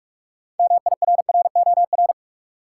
Morse code, heard as a keyed tone, about 45 wpm, 700 Hz. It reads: MIRROR
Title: Six Letter Words - 45wpm